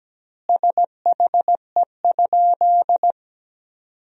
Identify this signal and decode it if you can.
Morse code, heard as a keyed tone, about 17 wpm, 700 Hz. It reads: SHE?
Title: 2x Word-Spacing - Sentences from Top 100 Words - 17wpm